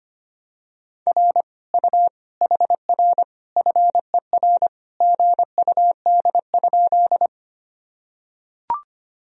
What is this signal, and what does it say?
Morse code, heard as a keyed tone, about 25 wpm, 700 Hz. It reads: R U HR FER GUD?
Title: Sentences from Top 200 Words and Abbreviations 25wpm